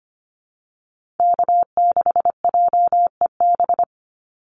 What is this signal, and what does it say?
Morse code, heard as a keyed tone, about 25 wpm, 700 Hz. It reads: K6JEB
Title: Call Signs - 25wpm